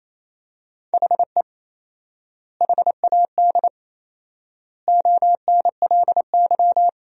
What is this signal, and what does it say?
Morse code, heard as a keyed tone, about 28 wpm, 700 Hz. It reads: HE HAD ONLY
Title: 4x Word-Spacing - Sentences from Top 300 Words - 28wpm